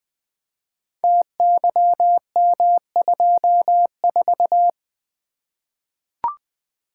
Morse code, heard as a keyed tone, about 20 wpm, 700 Hz. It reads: TYM24
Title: Code Groups - 5 Letter:Numbers - 20wpm